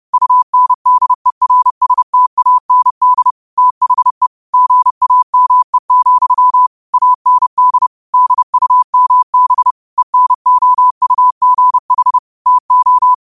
Morse code, heard as a keyed tone, about 30 wpm, 1000 Hz. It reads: UNDERSTAND THE GAME, AND DUMB ENOUGH TO
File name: qotdfiles / 30wpm